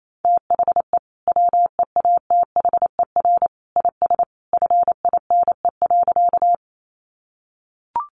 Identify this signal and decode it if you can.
Morse code, heard as a keyed tone, about 28 wpm, 700 Hz. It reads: THE WEATHER IS FINE.
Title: Sentences from Top 800 Words 28wpm